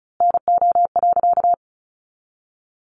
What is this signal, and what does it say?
Morse code, heard as a keyed tone, about 34 wpm, 700 Hz. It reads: NO.
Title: A Sound Of Thunder 35wpm